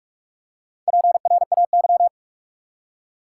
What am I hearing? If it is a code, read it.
Morse code, heard as a keyed tone, about 46 wpm, 700 Hz. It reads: PRAY
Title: Four Letter Words - 45wpm